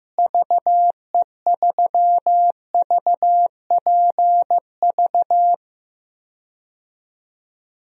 Morse code, heard as a keyed tone, about 15 wpm, 700 Hz. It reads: VE3VPV
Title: DXCC-100 Call Signs - 6 Characters - Repeated 3x - 15wpm